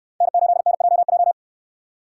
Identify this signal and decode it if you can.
Morse code, heard as a keyed tone, about 65 wpm, 700 Hz. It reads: N8NJJ